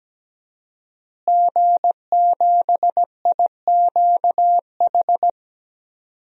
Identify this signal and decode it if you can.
Morse code, heard as a keyed tone, about 17 wpm, 700 Hz. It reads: G7IQH
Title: DXCC-100 Call Signs - 5 Characters - Repeated 2x - 17wpm